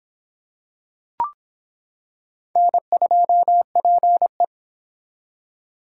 Morse code, heard as a keyed tone, about 26 wpm, 700 Hz. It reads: N2PE